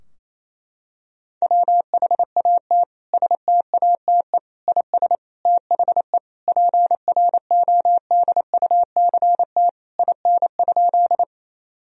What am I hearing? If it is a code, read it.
Morse code, heard as a keyed tone, about 28 wpm, 700 Hz. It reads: WHAT STATE IS THE PRODUCT IN?
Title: Sentences from Top 1000 Words - Review Part 2 - 28wpm